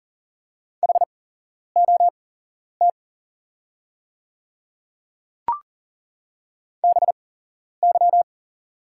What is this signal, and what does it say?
Morse code, heard as a keyed tone, about 39 wpm, spaced as for 11 wpm, 690 Hz. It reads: HOT BY
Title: 8x Character Spacing - Top 100 Words - 40wpm